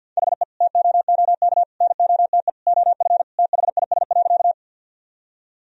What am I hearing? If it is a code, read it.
Morse code, heard as a keyed tone, about 50 wpm, 700 Hz. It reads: HE TOOK NOTE OF THIS.
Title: Sentences from Top 1000 Words - Review Part 2 - 50wpm